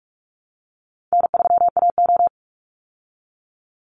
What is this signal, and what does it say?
Morse code, heard as a keyed tone, about 44 wpm, 700 Hz. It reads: N3AO
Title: Easy Call Signs Repeated 3x - 45wpm